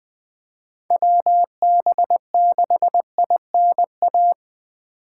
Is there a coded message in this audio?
Yes: Morse code from 0.9 to 4.3 s, heard as a keyed tone, about 20 wpm, 700 Hz. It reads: WB6INA